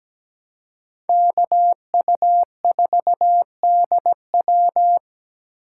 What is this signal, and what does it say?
Morse code, heard as a keyed tone, about 20 wpm, 700 Hz. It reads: KU4DW
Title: Speed Racing - USA 2x2 Call Signs 17wpm